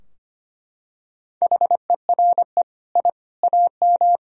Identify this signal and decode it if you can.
Morse code, heard as a keyed tone, about 25 wpm, 700 Hz. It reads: HERE I AM
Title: Sentences from Top 1000 Words - Review Part 2 - 25wpm